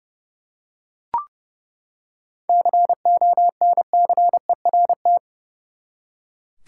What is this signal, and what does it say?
Morse code, heard as a keyed tone, about 30 wpm, 700 Hz. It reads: CONCERT
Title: Seven Letter Words - 30wpm